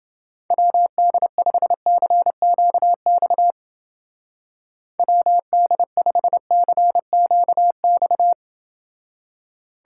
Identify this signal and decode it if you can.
Morse code, heard as a keyed tone, about 28 wpm, 700 Hz. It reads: WD5CQX WD5CQX